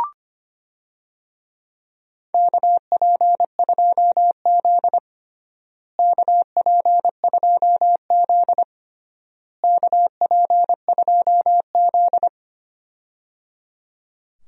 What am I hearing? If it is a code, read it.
Morse code, heard as a keyed tone, about 25 wpm, 700 Hz. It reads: KP2Z KP2Z KP2Z
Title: DXCC-100 Call Signs - 4 Characters - Repeated 3x - 25wpm